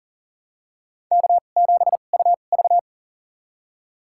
Morse code, heard as a keyed tone, about 40 wpm, spaced as for 29 wpm, 700 Hz. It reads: K7UV
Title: USA Call Signs - 4 Characters - 2x Character Spacing - 40wpm